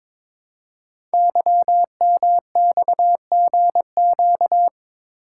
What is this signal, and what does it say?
Morse code, heard as a keyed tone, about 23 wpm, 700 Hz. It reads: YMXGQ
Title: Code Groups - 5 Letters - 22wpm